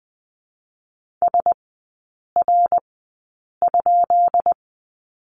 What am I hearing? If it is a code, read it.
Morse code, heard as a keyed tone, about 20 wpm, 700 Hz. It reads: S R ?
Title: 145 Lesson 24e - Letter y - Three Characters 20wpm